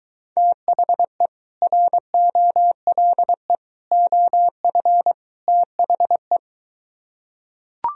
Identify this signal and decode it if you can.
Morse code, heard as a keyed tone, about 24 wpm, 700 Hz. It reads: THE ROLE OF THE